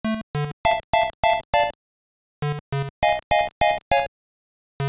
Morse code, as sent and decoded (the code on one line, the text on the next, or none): none